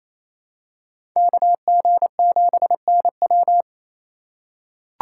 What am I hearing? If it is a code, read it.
Morse code, heard as a keyed tone, about 28 wpm, 700 Hz. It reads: KG7NW
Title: USA Call Signs - 5 Characters - 6x Character Spacing - 28wpm